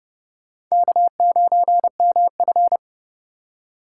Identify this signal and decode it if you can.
Morse code, heard as a keyed tone, about 37 wpm, 700 Hz. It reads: K9MF
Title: Speed Racing - USA 1x2 Call Signs 30wpm